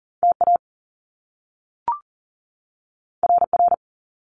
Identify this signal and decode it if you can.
Morse code, heard as a keyed tone, about 40 wpm, 700 Hz. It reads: TA RR
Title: Top Two Letter Combinations in Words 40wpm